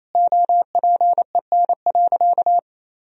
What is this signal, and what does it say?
Morse code, heard as a keyed tone, about 28 wpm, 700 Hz. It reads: OPEN.
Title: Sentences from Top 1000 Words - Review Part 1 - 28wpm